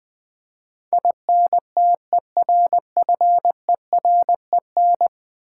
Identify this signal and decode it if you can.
Morse code, heard as a keyed tone, about 20 wpm, 700 Hz. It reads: INTERFEREN
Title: Build A Word - 5 Letters At A Time - 12-Letter Words - 20wpm